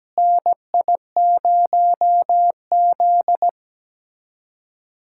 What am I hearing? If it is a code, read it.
Morse code, heard as a keyed tone, about 17 wpm, 700 Hz. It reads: NI0Z
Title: USA Callsign DE Callsign - 4 Characters - 17wpm